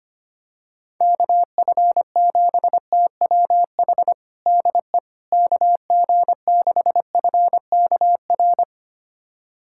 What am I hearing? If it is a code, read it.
Morse code, heard as a keyed tone, about 25 wpm, 700 Hz. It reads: KF7TWH DE KG6FKR
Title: USA Callsign DE Callsign - 6 Characters - 25wpm